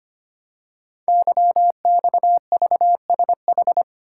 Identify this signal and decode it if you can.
Morse code, heard as a keyed tone, about 25 wpm, 700 Hz. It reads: YXVSH
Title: Code Groups - 5 Letter:Numbers - 25wpm